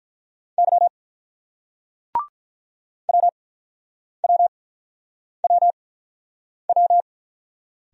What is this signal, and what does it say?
Morse code, heard as a keyed tone, about 49 wpm, spaced as for 9 wpm, 700 Hz. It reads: Y WWWW